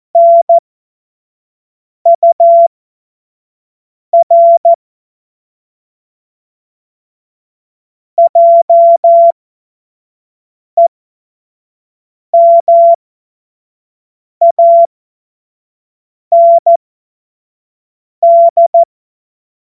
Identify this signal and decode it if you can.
Morse code, heard as a keyed tone, about 14 wpm, spaced as for 5 wpm, 680 Hz. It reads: NUR JEMAND
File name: Morsekurs / Witze / Witz 27. Juni 2023 25 F70